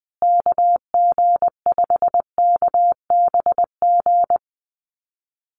Morse code, heard as a keyed tone, about 20 wpm, 700 Hz. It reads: KG5KBG
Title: USA Call Signs Repeated 3x 20wpm